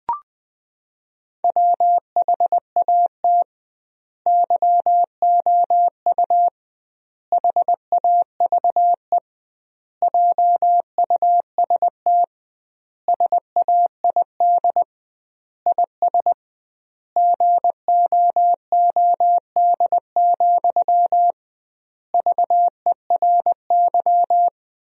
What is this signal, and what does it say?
Morse code, heard as a keyed tone, about 20 wpm, 700 Hz. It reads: WHAT YOU HAVE JUST SAID IS GOOD, VERY
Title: 2x Word-Spacing - Sentences from Top 200 Words - 20wpm